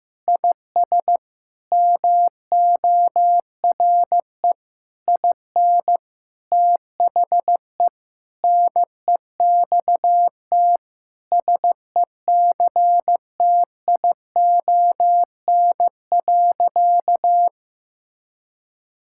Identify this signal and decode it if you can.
Morse code, heard as a keyed tone, about 15 wpm, 700 Hz. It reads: IS MORE IN THE NEXT SECTION.
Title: Sentences from Top 1000 Words - Review Part 5 - 15wpm